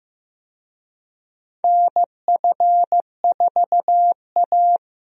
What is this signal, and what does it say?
Morse code, heard as a keyed tone, about 15 wpm, 700 Hz. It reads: NF4A
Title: USA Call Signs - 4 Characters - Repeated 2x - 15wpm